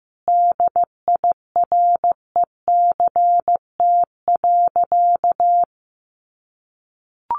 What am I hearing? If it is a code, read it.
Morse code, heard as a keyed tone, about 14 wpm, 700 Hz. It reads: DIRECT.
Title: Sentences from Top 1500 Words 15wpm